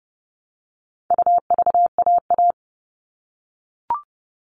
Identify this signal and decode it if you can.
Morse code, heard as a keyed tone, about 30 wpm, 700 Hz. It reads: UVAA
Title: All Cut Numbers - 4-Digit 30wpm